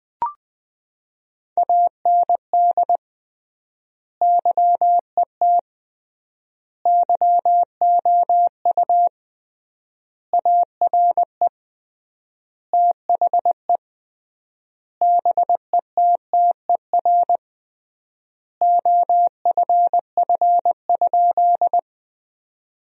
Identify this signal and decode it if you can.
Morse code, heard as a keyed tone, about 20 wpm, 700 Hz. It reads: AND YET YOU ARE THE BETTER OFF?
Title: 3x Word-Spacing - Sentences from Top 500 Words - 20wpm